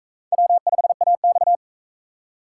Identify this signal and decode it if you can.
Morse code, heard as a keyed tone, about 41 wpm, 700 Hz. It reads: W5AX